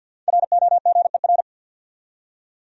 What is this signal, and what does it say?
Morse code, heard as a keyed tone, about 52 wpm, 700 Hz. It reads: ROGER
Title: USA Common Names - Max 5 Letters - 50wpm